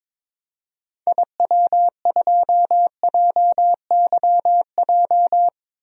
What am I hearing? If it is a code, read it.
Morse code, heard as a keyed tone, about 22 wpm, 700 Hz. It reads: IW2JYJ
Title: DXCC-100 Call Signs - 6 Characters - 22wpm